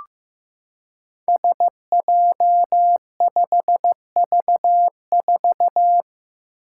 Morse code, heard as a keyed tone, about 15 wpm, 700 Hz. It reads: SJ5V4
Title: Five Challenging Letter-Number Combinations 15wpm